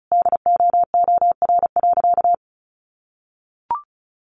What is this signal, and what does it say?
Morse code, heard as a keyed tone, about 34 wpm, 700 Hz. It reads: DOOR.
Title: Sentences from Top 400 Words 35wpm